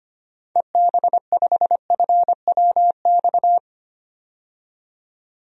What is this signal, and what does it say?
Morse code, heard as a keyed tone, about 25 wpm, 700 Hz. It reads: EB5FWX
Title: DXCC-100 Call Signs - 6 Characters - 25wpm